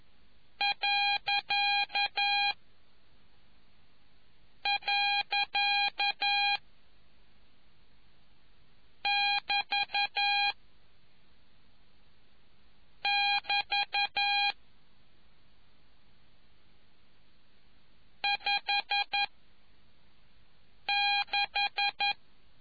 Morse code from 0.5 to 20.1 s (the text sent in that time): . <BT> <BT> 5